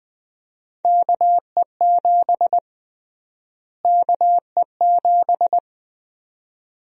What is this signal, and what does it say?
Morse code, heard as a keyed tone, about 20 wpm, 700 Hz. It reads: KE7 KE7
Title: USA Call Sign Prefixes - Repeated 2x - 20wpm